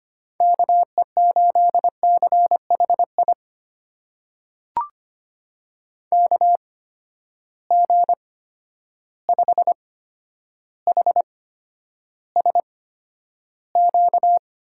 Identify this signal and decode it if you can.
Morse code, heard as a keyed tone, about 25 wpm, spaced as for 11 wpm, 700 Hz. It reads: KE8CHI KG5HSQ
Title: USA Call Signs - 6 Characters - 8x Character Spacing - 25wpm